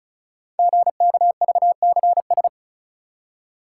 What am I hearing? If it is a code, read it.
Morse code, heard as a keyed tone, about 36 wpm, 690 Hz. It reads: GKVCS
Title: Code Groups - 5 Letters - 35wpm